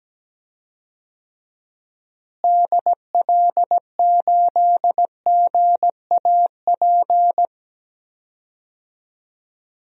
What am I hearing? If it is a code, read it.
Morse code, heard as a keyed tone, about 17 wpm, 700 Hz. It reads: DL8GAP
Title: DXCC-100 Call Signs - 6 Characters - Repeated 2x - 17wpm